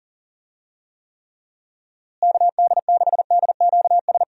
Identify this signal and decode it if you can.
Morse code, heard as a keyed tone, about 37 wpm, 700 Hz. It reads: KD6DQS
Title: USA Call Signs - 6 Characters - Repeated 2x - 40wpm